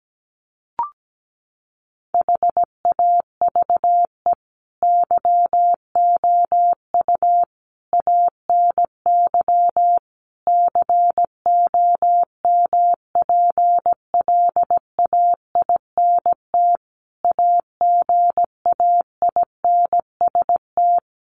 Morse code, heard as a keyed tone, about 17 wpm, 700 Hz. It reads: HAVE YOU ANY COMPLAINT AGAINST
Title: Sentences from Top 1400 Words 17wpm